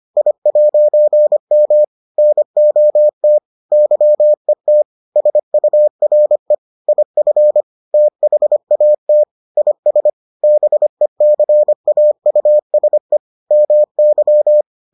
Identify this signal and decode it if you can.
Morse code, heard as a keyed tone, about 25 wpm, 600 Hz. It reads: I'M NOT YET SURE IF THAT IS BECAUSE MY